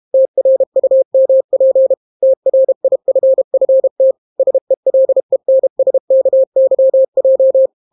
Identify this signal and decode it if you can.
Morse code, heard as a keyed tone, about 32 wpm, 530 Hz. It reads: TRUMP TRIFFT SELENSKYJ